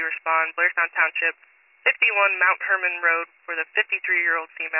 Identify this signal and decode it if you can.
Emergency Alert System (SAME) tones, heard as harsh buzzing bursts, with speech. It is heard from 0.6 to 4.4 s.